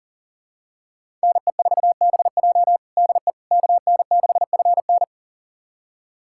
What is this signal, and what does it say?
Morse code, heard as a keyed tone, about 40 wpm, 700 Hz. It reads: NE4BJ DE KN6FN
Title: USA Callsign DE Callsign - 5 Characters - 40wpm